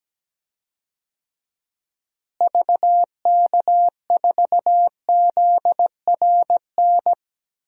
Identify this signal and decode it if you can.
Morse code, heard as a keyed tone, about 17 wpm, 700 Hz. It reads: VK4ZRN
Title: DXCC-100 Call Signs - 6 Characters - Repeated 2x - 17wpm